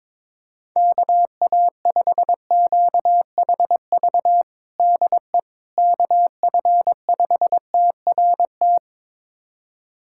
Morse code, heard as a keyed tone, about 22 wpm, 700 Hz. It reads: KA5QHV DE KF5TRT
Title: USA Callsign DE Callsign - 6 Characters - 22wpm